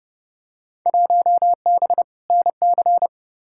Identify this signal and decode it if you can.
Morse code, heard as a keyed tone, about 30 wpm, 700 Hz. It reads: 1B NC
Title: Field Day Exchanges Encore - 30wpm